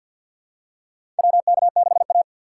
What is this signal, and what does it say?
Morse code, heard as a keyed tone, about 52 wpm, 700 Hz. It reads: WK6A
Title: USA Call Signs - 4 Characters - 6x Character Spacing - 50wpm